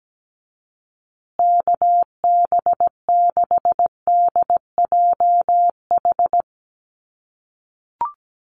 Morse code, heard as a keyed tone, about 17 wpm, 700 Hz. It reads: KB6DJH
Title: USA Call Signs Repeated 3x 17wpm